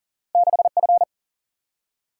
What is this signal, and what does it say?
Morse code, heard as a keyed tone, about 42 wpm, 700 Hz. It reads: 6F
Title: Single-Letter Single-Number 40wpm